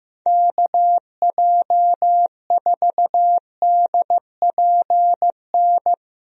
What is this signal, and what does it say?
Morse code, heard as a keyed tone, about 15 wpm, 700 Hz. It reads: KJ4DPN
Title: USA Call Signs - 6 Characters - Repeated 3x - 15wpm